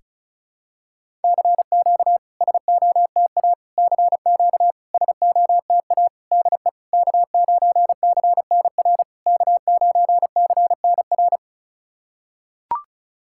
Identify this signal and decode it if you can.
Morse code, heard as a keyed tone, about 34 wpm, 690 Hz. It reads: CQ SOTA CQ SOTA DE K9CNR K9CNR
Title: SOTA QSO Practice 35wpm